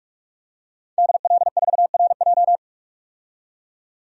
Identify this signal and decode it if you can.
Morse code, heard as a keyed tone, about 44 wpm, 700 Hz. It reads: DL4RJ
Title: DXCC-100 Call Signs - 5 Characters - 45wpm